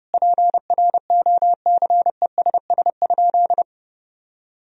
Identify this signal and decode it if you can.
Morse code, heard as a keyed tone, about 32 wpm, 700 Hz. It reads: PROCESS?
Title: Sentences from Top 1000 Words - Review Part 3 - 30wpm